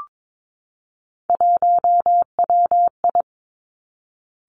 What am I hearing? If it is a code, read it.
Morse code, heard as a keyed tone, about 22 wpm, 700 Hz. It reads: 1WI
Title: Three Challenging Letter-Number Combinations 22wpm